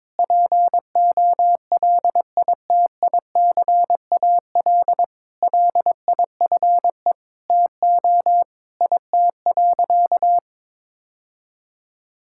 Morse code, heard as a keyed tone, about 22 wpm, 700 Hz. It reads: POLITICAL LIFE TO IT.
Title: Sentences from Top 1000 Words - Review Part 5 - 22wpm